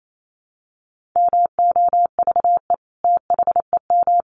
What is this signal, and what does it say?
Morse code, heard as a keyed tone, about 28 wpm, 700 Hz. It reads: MOVE THEM
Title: Sentences from Top 1400 Words 28wpm